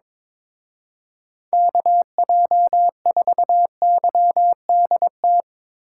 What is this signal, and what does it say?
Morse code, heard as a keyed tone, about 24 wpm, 700 Hz. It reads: KJ4YDT
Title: Speed Racing - USA 2x3 Call Signs 22wpm